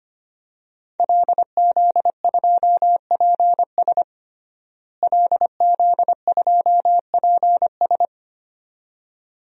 Morse code, heard as a keyed tone, about 25 wpm, 700 Hz. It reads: LZ2PS LZ2PS
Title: DXCC-100 Call Signs - 5 Characters - Repeated 2x - 25wpm